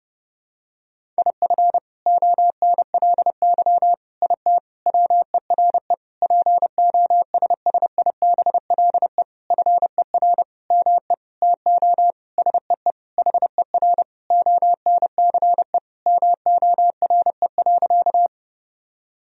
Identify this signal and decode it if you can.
Morse code, heard as a keyed tone, about 29 wpm, 700 Hz. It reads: IF ONLY IT WERE POSSIBLE FER ME TO SEE HER ONCE MORE.
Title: Sentences from Top 500 Words and Abbreviations 30wpm